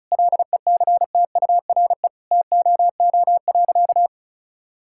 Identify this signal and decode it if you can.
Morse code, heard as a keyed tone, about 34 wpm, 700 Hz. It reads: LECTURE TOO.
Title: Sentences from Top 1600 Words 35wpm